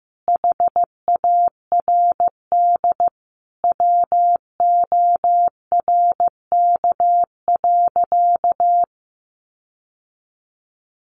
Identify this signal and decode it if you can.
Morse code, heard as a keyed tone, about 15 wpm, 700 Hz. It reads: HARD WORK.
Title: Sentences from Top 1300 Words 15wpm